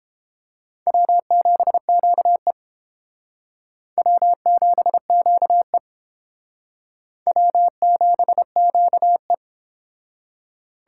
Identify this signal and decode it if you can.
Morse code, heard as a keyed tone, about 29 wpm, 700 Hz. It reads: W7QE W7QE W7QE